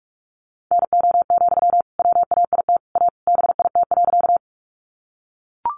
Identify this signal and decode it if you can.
Morse code, heard as a keyed tone, about 44 wpm, 700 Hz. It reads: NO, WAIT A BIT.
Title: Sentences from Top 900 Words 45wpm